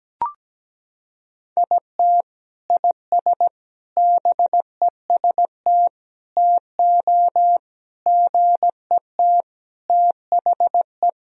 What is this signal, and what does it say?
Morse code, heard as a keyed tone, about 17 wpm, 700 Hz. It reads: IT IS BEST TO GET THE
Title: Sentences from Top 1000 Words - Review Part 1 - 17wpm